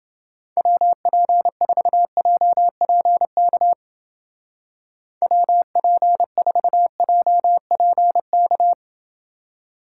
Morse code, heard as a keyed tone, about 28 wpm, 700 Hz. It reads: WP4JPK WP4JPK